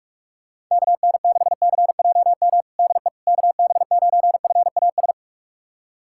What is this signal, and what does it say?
Morse code, heard as a keyed tone, about 45 wpm, 690 Hz. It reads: KN6CJM DE KB9FAS
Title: USA Callsign DE Callsign 45wpm